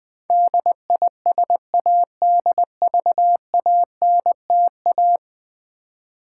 Morse code, heard as a keyed tone, about 20 wpm, 700 Hz. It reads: DISADVANTA
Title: Build A Word - 5 Letters At A Time - 12-Letter Words - 20wpm